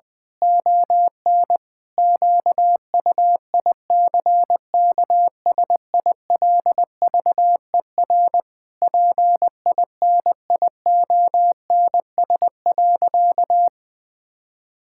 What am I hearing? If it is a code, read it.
Morse code, heard as a keyed tone, about 20 wpm, 700 Hz. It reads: ON QUICKSILVER PINIONS.
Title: Pangram Challenge v2 - 20wpm